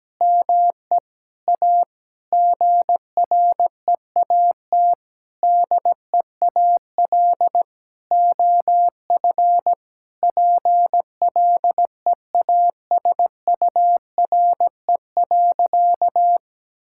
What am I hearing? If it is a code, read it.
Morse code, heard as a keyed tone, about 17 wpm, 700 Hz. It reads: ME A GREAT DEAL OF PLEASURE.
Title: Sentences from Top 1600 Words 17wpm